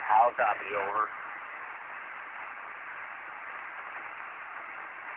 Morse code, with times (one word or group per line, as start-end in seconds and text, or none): none